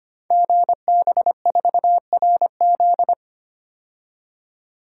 Morse code, heard as a keyed tone, about 25 wpm, 700 Hz. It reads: GB4RZ
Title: Code Groups - 5 Letter:Numbers - 25wpm